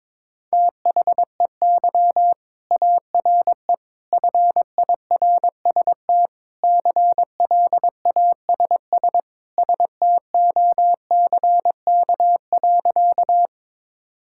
Morse code, heard as a keyed tone, about 22 wpm, 700 Hz. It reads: THEY ARE FIRST CLASS STOCK.
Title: Sentences from Top 1000 Words - Review Part 5 - 22wpm